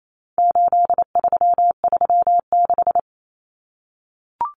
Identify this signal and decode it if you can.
Morse code, heard as a keyed tone, about 28 wpm, 700 Hz. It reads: 8336
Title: Numbers - 4-Digit 28wpm